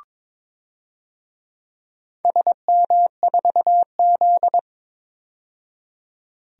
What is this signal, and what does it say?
Morse code, heard as a keyed tone, about 21 wpm, 700 Hz. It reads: SM4Z
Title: DXCC-100 Call Signs - 4 Characters - 22wpm